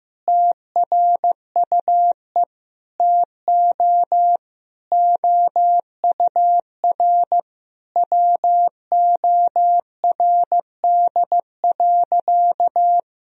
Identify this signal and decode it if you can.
Morse code, heard as a keyed tone, about 15 wpm, 700 Hz. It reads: TRUE TO OUR WORD.
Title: Sentences from Top 1000 Words - Review Part 3 - 15wpm